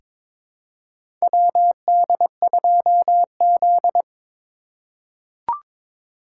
Morse code, heard as a keyed tone, about 22 wpm, 700 Hz. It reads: WD2Z
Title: USA Call Signs - 4x Character Spacing - 22wpm